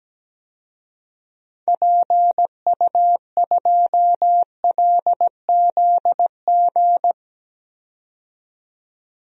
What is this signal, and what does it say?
Morse code, heard as a keyed tone, about 17 wpm, 700 Hz. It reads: PU2LZG
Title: DXCC-100 Call Signs - 6 Characters - 17wpm